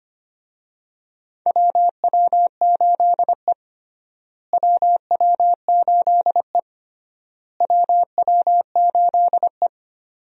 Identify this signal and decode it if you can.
Morse code, heard as a keyed tone, about 25 wpm, 700 Hz. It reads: WW8E WW8E WW8E
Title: USA Call Signs - 4 Characters - Repeated 3x - 25wpm